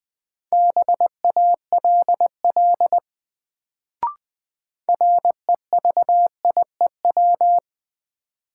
Morse code, heard as a keyed tone, about 21 wpm, 700 Hz. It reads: BALL REVIEW
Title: Sentences from Top 900 Words - New Word Drill - 20wpm